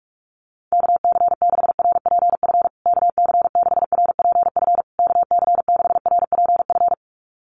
Morse code, heard as a keyed tone, about 46 wpm, 690 Hz. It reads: KC6RPF KC6RPF KC6RPF
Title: USA Call Signs Repeated 3x 45wpm